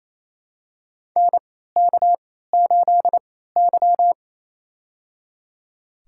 Morse code, heard as a keyed tone, about 28 wpm, spaced as for 16 wpm, 700 Hz. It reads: NK8Y
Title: USA Call Signs - 4 Characters - 3x Character Spacing - 28wpm